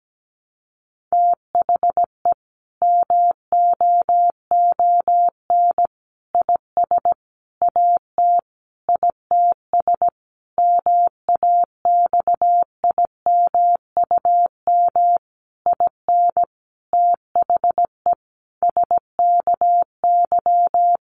Morse code, heard as a keyed tone, about 17 wpm, 700 Hz. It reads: THE MOON IS AT ITS MAXIMUM IN THE SKY
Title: Sentences from Top 1300 Words 17wpm